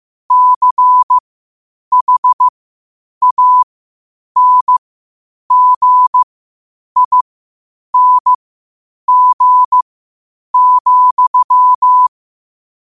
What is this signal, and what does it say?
Morse code, heard as a keyed tone, about 15 wpm, spaced as for 8 wpm, 1000 Hz. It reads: CHANGING,
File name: qotdfiles / 8wpm